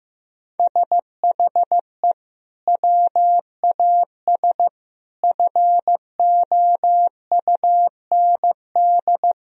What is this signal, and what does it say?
Morse code, heard as a keyed tone, about 15 wpm, 700 Hz. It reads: SHE WAS FOUND
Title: Sentences from Top 1000 Words - Review Part 3 - 15wpm